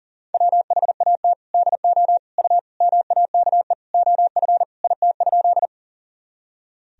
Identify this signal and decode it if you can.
Morse code, heard as a keyed tone, about 40 wpm, 700 Hz. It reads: WHAT DO U MAKE OF IT?
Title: Sentences from Top 100 Words and Abbreviations 40wpm